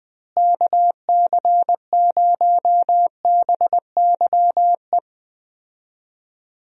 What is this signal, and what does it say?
Morse code, heard as a keyed tone, about 20 wpm, 700 Hz. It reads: KC0BYE
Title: USA Call Signs - 6 Characters - Repeated 2x - 20wpm